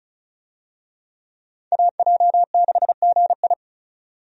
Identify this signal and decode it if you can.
Morse code, heard as a keyed tone, about 36 wpm, 700 Hz. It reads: AJ6GI
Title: USA Call Signs - 5 Characters - 35wpm